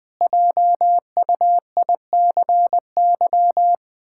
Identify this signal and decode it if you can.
Morse code, heard as a keyed tone, about 20 wpm, 700 Hz. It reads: JUICY
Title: Pangram Challenge v2 - 20wpm